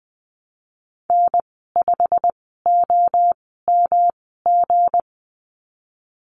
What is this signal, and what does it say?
Morse code, heard as a keyed tone, about 20 wpm, spaced as for 15 wpm, 700 Hz. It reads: N5OMG
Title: 173 Lesson 28g - Number 9 - Call Signs 20wpm